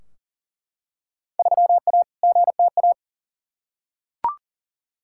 Morse code, heard as a keyed tone, about 40 wpm, 700 Hz. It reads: 3A GTA
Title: Field Day Exchanges - 40wpm